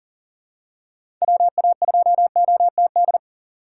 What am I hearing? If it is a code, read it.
Morse code, heard as a keyed tone, about 40 wpm, 700 Hz. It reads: WA2OTD
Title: USA Call Signs - 6 Characters - 3x Character Spacing - 40wpm